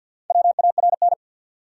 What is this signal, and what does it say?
Morse code, heard as a keyed tone, about 51 wpm, 700 Hz. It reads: WARN
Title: Four Letter Words - 50wpm